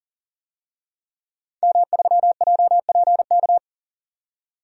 Morse code, heard as a keyed tone, about 40 wpm, 700 Hz. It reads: M3JPK
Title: DXCC-100 Call Signs - 5 Characters - 40wpm